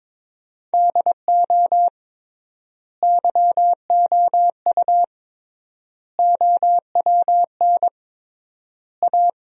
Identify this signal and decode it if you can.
Morse code, heard as a keyed tone, about 21 wpm, 700 Hz. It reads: DO YOU OWN A
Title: 3x Word-Spacing - Sentences from Top 400 Words - 22wpm